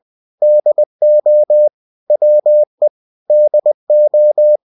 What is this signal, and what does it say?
Morse code, heard as a keyed tone, about 20 wpm, 600 Hz. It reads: DO WE DO
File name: cw-books / winnie-the-pooh / winnie-20wpm-page-0030